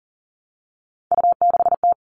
Morse code, heard as a keyed tone, about 40 wpm, 700 Hz. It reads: U6T
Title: All Cut Numbers - 3-Digit 40wpm